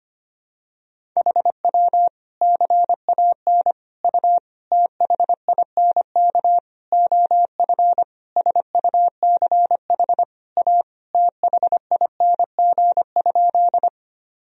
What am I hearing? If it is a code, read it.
Morse code, heard as a keyed tone, about 25 wpm, 700 Hz. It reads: HW CAN U THINK OF SUCH A THING?
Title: Sentences from Top 200 Words and Abbreviations 25wpm